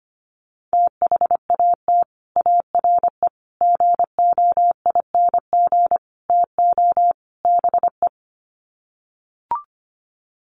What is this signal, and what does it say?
Morse code, heard as a keyed tone, about 25 wpm, 700 Hz. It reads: THAT ARE GOING TO BE
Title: Sets of 5 Words Encore2 25wpm